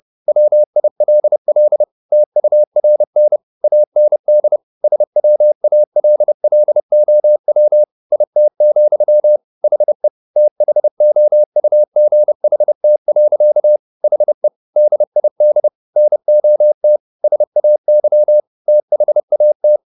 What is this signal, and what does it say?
Morse code, heard as a keyed tone, about 30 wpm, 600 Hz. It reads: WILL TURN AND SWALLOW IT, HE THOUGHT. HE DID NOT SAY THAT
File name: the old man and the sea / the old man and the sea 1 30wpm